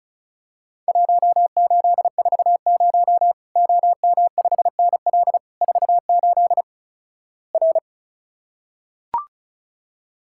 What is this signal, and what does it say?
Morse code, heard as a keyed tone, about 35 wpm, 700 Hz. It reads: 1840 OM5NL 48 R
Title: Worked All Europe DX Contest - QTC Exchange - 35wpm